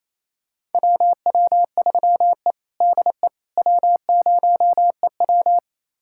Morse code, heard as a keyed tone, about 27 wpm, 700 Hz. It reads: WW3E DE W0EW
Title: USA Callsign DE Callsign - 4 Characters - 28wpm